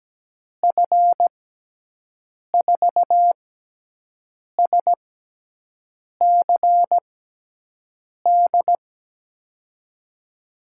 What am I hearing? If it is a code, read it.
Morse code, heard as a keyed tone, about 17 wpm, 700 Hz. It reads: F 4 S C D
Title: USA Call Signs - 6 Characters - 6x Character Spacing - 17wpm